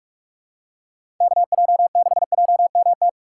Morse code, heard as a keyed tone, about 46 wpm, 700 Hz. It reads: KJ6JMT
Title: USA Call Signs - 6 Characters - 3x Character Spacing - 45wpm